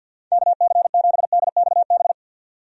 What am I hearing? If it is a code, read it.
Morse code, heard as a keyed tone, about 46 wpm, 700 Hz. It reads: KC7DXB
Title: USA Call Signs - 6 Characters - 50wpm